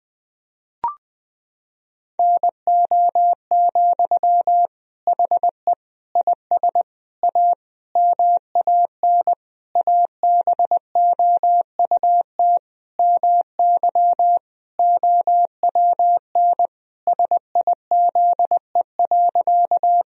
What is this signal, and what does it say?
Morse code, heard as a keyed tone, about 20 wpm, 700 Hz. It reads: NO, HE IS A MAN ABOUT MY OWN SIZE.
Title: Sentences from Top 1000 Words - Review Part 2 - 20wpm